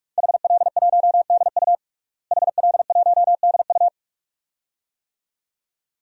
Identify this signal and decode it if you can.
Morse code, heard as a keyed tone, about 46 wpm, 700 Hz. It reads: HL1DU HL1DU
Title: DXCC-100 Call Signs - 5 Characters - Repeated 2x - 45wpm